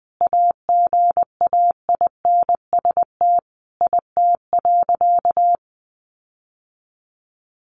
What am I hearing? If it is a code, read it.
Morse code, heard as a keyed tone, about 20 wpm, 700 Hz. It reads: AGAINST IT.
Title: Sentences from Top 1500 Words 20wpm